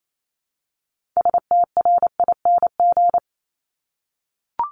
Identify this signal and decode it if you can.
Morse code, heard as a keyed tone, about 28 wpm, 700 Hz. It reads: STRING
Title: Multi-letter Phonemes 28wpm